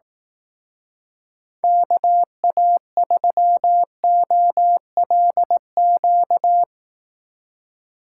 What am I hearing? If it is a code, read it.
Morse code, heard as a keyed tone, about 18 wpm, 700 Hz. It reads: KA3OLQ
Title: Speed Racing - USA 2x3 Call Signs 15wpm